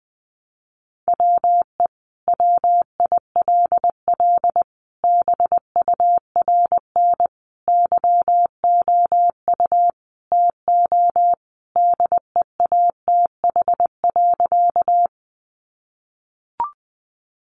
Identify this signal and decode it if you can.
Morse code, heard as a keyed tone, about 20 wpm, 700 Hz. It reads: WE WILL BURN YOU TO DEATH.
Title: Sentences from Top 1300 Words 20wpm